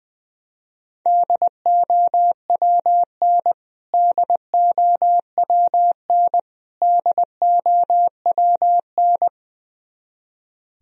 Morse code, heard as a keyed tone, about 19 wpm, 700 Hz. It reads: DOWN DOWN DOWN
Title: Top 100 Words Repeated 20wpm